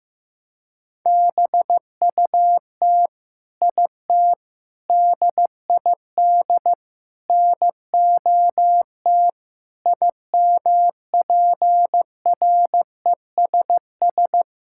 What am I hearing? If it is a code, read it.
Morse code, heard as a keyed tone, about 15 wpm, 700 Hz. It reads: BUT IT DID NOT IMPRESS
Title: Sentences from Top 1600 Words 15wpm